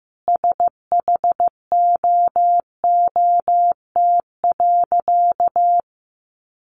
Morse code, heard as a keyed tone, about 15 wpm, 700 Hz. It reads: SHOOT.
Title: Sentences from Top 1200 Words 15wpm